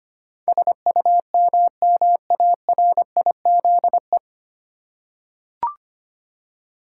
Morse code, heard as a keyed tone, about 25 wpm, 700 Hz. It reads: SUMMARIZE
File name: practice / 25wpm / 9 Letter Words - First 3 Letters Given - 25wpm